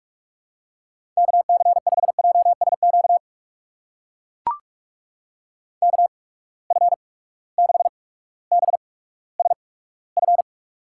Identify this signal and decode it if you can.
Morse code, heard as a keyed tone, about 44 wpm, spaced as for 19 wpm, 700 Hz. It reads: KC5JSQ KF6BSF
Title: USA Call Signs - 6 Characters - 8x Character Spacing - 45wpm